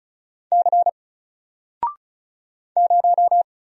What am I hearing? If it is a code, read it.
Morse code, heard as a keyed tone, about 36 wpm, 700 Hz. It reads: C 0
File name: Single-Letter-Number-35wpm